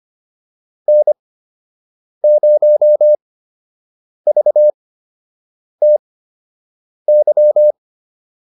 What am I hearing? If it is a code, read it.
Morse code, heard as a keyed tone, about 25 wpm, spaced as for 7 wpm, 600 Hz. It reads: N0VTY